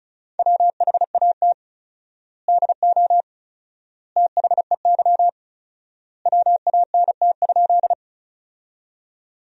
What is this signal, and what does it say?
Morse code, heard as a keyed tone, about 36 wpm, 700 Hz. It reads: WHAT DO THEY WANT?
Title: 4x Word-Spacing - Sentences from Top 200 Words - 35wpm